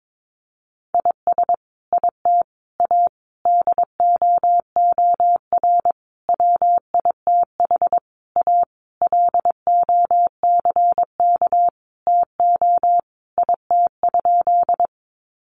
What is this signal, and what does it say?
Morse code, heard as a keyed tone, about 22 wpm, 700 Hz. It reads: IS IT A DOOR WITH A LOCK TO IT?
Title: Sentences from Top 1300 Words 22wpm